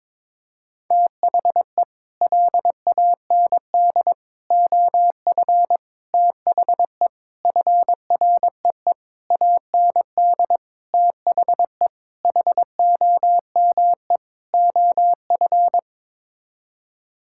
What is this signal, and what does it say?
Morse code, heard as a keyed tone, about 22 wpm, 700 Hz. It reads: THE LAND OF THE FREE AND THE HOME OF
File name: Sets of 9 Words - 22wpm